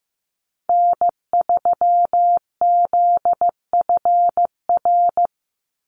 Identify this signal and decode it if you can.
Morse code, heard as a keyed tone, about 15 wpm, 700 Hz. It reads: N3ZFR
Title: USA Call Signs 15wpm